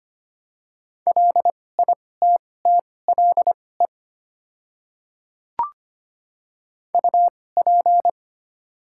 Morse code, heard as a keyed tone, about 25 wpm, spaced as for 18 wpm, 700 Hz. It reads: LITTLE UP
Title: 2x Character Spacing - Top 200 Words - 25wpm